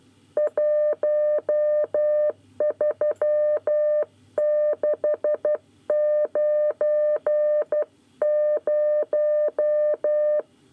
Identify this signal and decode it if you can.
Morse code, heard as a keyed tone, about 12 wpm, 600 Hz. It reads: 13690
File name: 9 lezione cw 1